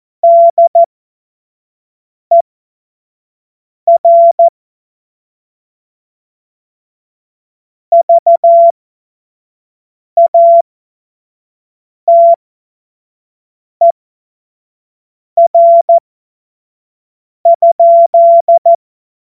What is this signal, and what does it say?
Morse code, heard as a keyed tone, about 14 wpm, spaced as for 5 wpm, 680 Hz. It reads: DER VATER?
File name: Morsekurs / Witze / Witz 31. August 2021, 25 F70